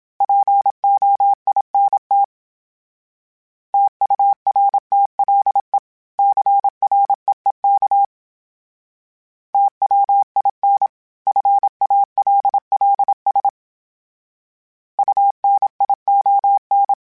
Morse code, heard as a keyed tone, about 27 wpm, 800 Hz. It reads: POINT TURTLE CREEK TWIN FALLS UNION
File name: fromSMs / FromSMs mp3s / Cities@25